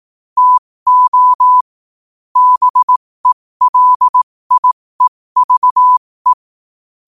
Morse code, heard as a keyed tone, about 18 wpm, spaced as for 15 wpm, 1000 Hz. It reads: TO BELIEVE
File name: qotdfiles / 15wpm